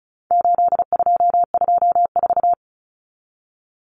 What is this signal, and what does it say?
Morse code, heard as a keyed tone, about 34 wpm, 700 Hz. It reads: 8224
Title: Common Cut Numbers - 4-Digit 35wpm